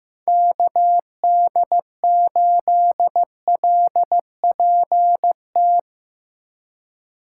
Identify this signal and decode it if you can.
Morse code, heard as a keyed tone, about 15 wpm, 700 Hz. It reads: KD8LPT
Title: USA Call Signs - 6 Characters - 4x Character Spacing - 15wpm